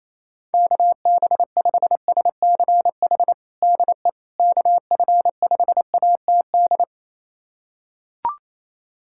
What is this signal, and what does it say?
Morse code, heard as a keyed tone, about 28 wpm, 700 Hz. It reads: KB5SCH DE KF5ATD
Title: USA Callsign DE Callsign - 6 Characters - 28wpm